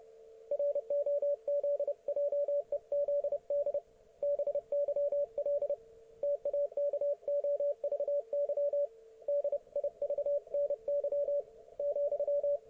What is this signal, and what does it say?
Morse code, heard as a keyed tone, about 30 wpm, 570 Hz, with radio noise. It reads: ROZJEZD BYL TAKOVY DIVNY,
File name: ok1fhi